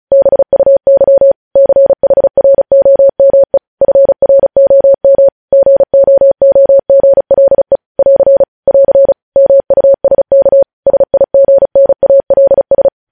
Morse code, heard as a keyed tone, about 36 wpm, 550 Hz. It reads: BUY CHROME FROM GOOGLE <AR> <AR> MUSK SIGNALS
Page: News Headlines in Morse Code at 35 WPM